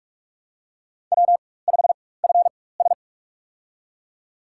Morse code, heard as a keyed tone, about 46 wpm, spaced as for 21 wpm, 700 Hz. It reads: W5FS
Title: USA Call Signs - 4 Characters - 4x Character Spacing - 45wpm